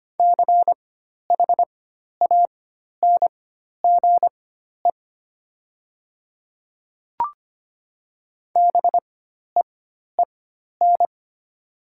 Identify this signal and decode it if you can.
Morse code, heard as a keyed tone, about 25 wpm, spaced as for 12 wpm, 700 Hz. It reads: CHANGE BEEN
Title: 4x Character Spacing - Top 200 Words - 25wpm